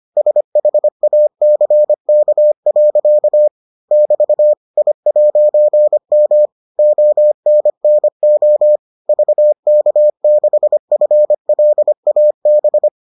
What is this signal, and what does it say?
Morse code, heard as a keyed tone, about 25 wpm, 600 Hz. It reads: SHACK. <BT> I'M ONNO VK6FLAB